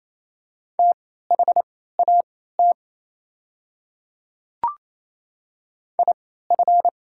Morse code, heard as a keyed tone, about 28 wpm, spaced as for 16 wpm, 700 Hz. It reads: THAT IF
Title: 3x Character Spacing - Top 100 Words - 28wpm